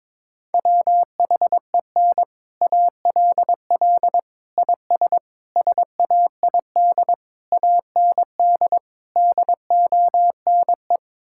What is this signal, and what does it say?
Morse code, heard as a keyed tone, about 22 wpm, 700 Hz. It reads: WHEN ALL IS SAID AND DONE
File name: practice / 22wpm / Sets of 6 Words - 22wpm